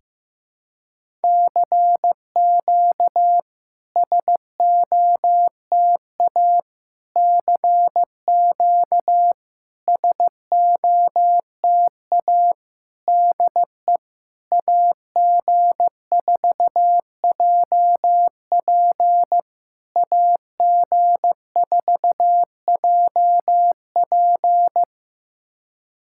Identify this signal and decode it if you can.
Morse code, heard as a keyed tone, about 16 wpm, 700 Hz. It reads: CQ SOTA CQ SOTA DE AG4JP AG4JP
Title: SOTA QSO Practice 15wpm